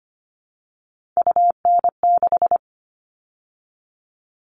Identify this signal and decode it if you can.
Morse code, heard as a keyed tone, about 25 wpm, 700 Hz. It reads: UN6
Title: All Cut Numbers - 3-Digit 25wpm